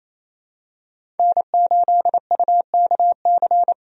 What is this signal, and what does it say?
Morse code, heard as a keyed tone, about 28 wpm, 700 Hz. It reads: N8UKC
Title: USA Call Signs - 5 Characters - 28wpm